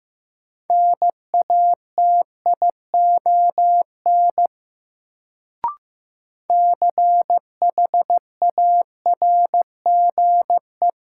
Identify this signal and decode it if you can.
Morse code, heard as a keyed tone, about 15 wpm, 700 Hz. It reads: NATION CHARGE
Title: Sentences from Top 900 Words - New Word Drill - 15wpm